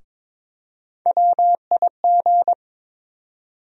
Morse code, heard as a keyed tone, about 22 wpm, 700 Hz. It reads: WIG
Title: USA Call Sign Postfixes - 3 Letters 22wpm